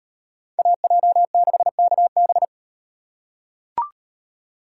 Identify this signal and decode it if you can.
Morse code, heard as a keyed tone, about 38 wpm, 700 Hz. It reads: AJ6KB